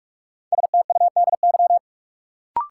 Morse code, heard as a keyed tone, about 46 wpm, 700 Hz. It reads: STUDY
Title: Sets of 5 Words - New Word Drill - 45wpm